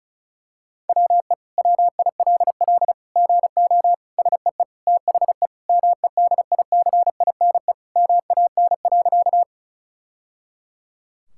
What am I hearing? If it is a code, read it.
Morse code, heard as a keyed tone, about 36 wpm, 700 Hz. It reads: WE WILL GO SEE THE MEDICINE MAN.
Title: Sentences from Top 1000 Words - Review Part 6 - 35wpm